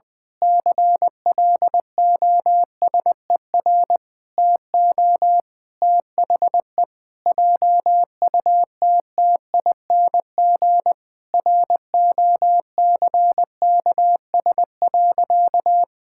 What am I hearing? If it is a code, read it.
Morse code, heard as a keyed tone, about 20 wpm, 700 Hz. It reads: CLOSER TO THE JUTTING ROCKS.
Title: Pangram Challenge v2 - 20wpm